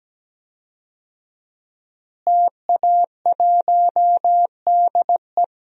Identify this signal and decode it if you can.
Morse code, heard as a keyed tone, about 17 wpm, 700 Hz. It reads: TA1DE
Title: DXCC-100 Call Signs - 5 Characters - Repeated 2x - 17wpm